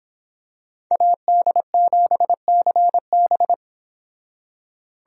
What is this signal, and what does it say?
Morse code, heard as a keyed tone, about 26 wpm, 700 Hz. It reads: AD7CB